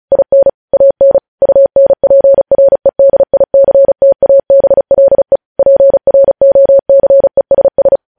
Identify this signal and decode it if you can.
Morse code, heard as a keyed tone, about 36 wpm, 550 Hz. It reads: IN AN UNPREDICTABLE PROCESS
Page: News Headlines in Morse Code at 35 WPM